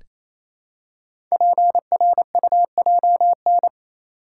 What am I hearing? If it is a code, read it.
Morse code, heard as a keyed tone, about 27 wpm, 700 Hz. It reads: PRUJN
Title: Code Groups - 5 Letters - 28wpm